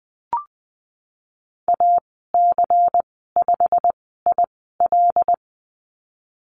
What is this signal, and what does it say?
Morse code, heard as a keyed tone, about 20 wpm, spaced as for 15 wpm, 700 Hz. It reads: AC5IL
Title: 147 Lesson 24g - Letter y - Call Signs 20wpm